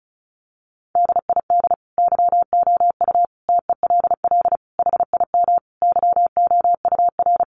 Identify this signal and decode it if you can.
Morse code, heard as a keyed tone, about 34 wpm, 700 Hz. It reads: DID YOU TELL HIM YOUR
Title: Sentences from Top 400 Words 35wpm